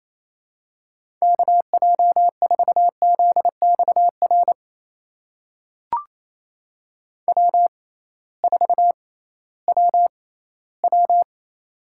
Morse code, heard as a keyed tone, about 28 wpm, spaced as for 16 wpm, 700 Hz. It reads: KJ4ZXR W4WW
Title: USA Call Signs - 6x Character Spacing - 28wpm